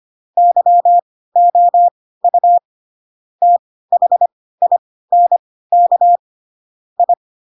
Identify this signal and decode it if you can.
Morse code, heard as a keyed tone, about 25 wpm, spaced as for 16 wpm, 700 Hz. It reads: YOU THINK I
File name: Pinocchio / Pinocchio ch2 10 25 ik1hge